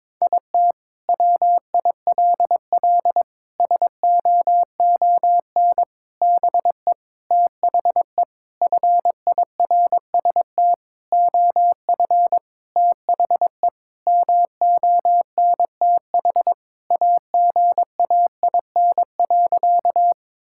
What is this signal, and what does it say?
Morse code, heard as a keyed tone, about 22 wpm, 700 Hz. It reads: IT WILL SOON BE THE FIRST OF THE MONTH AGAIN.
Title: Sentences from Top 1000 Words - Review Part 5 - 22wpm